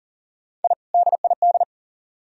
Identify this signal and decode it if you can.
Morse code, heard as a keyed tone, about 51 wpm, 700 Hz. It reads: I DID